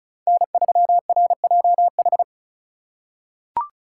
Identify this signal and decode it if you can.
Morse code, heard as a keyed tone, about 36 wpm, 700 Hz. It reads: N3RJH
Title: USA Call Signs - 5 Characters - 2x Character Spacing - 35wpm